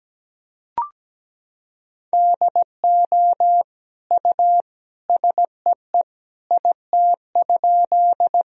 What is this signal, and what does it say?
Morse code, heard as a keyed tone, about 17 wpm, 700 Hz. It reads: DO U SEE IT?
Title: Sentences from Top 100 Words and Abbreviations 17wpm